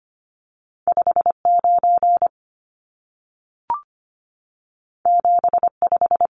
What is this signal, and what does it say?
Morse code, heard as a keyed tone, about 25 wpm, 700 Hz. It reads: 59 75
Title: Numbers - 2-Digit 25wpm